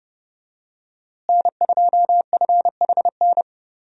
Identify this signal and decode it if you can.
Morse code, heard as a keyed tone, about 30 wpm, 700 Hz. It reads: N2FHN
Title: USA Call Signs - 5 Characters - Repeated 2x - 30wpm